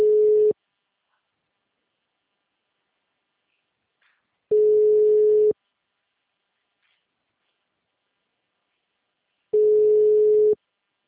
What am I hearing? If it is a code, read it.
Ringback tone (Europe, 425 Hz).